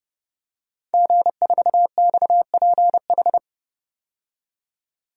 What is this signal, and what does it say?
Morse code, heard as a keyed tone, about 30 wpm, 700 Hz. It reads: G4XPH
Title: DXCC-100 Call Signs - 5 Characters - 30wpm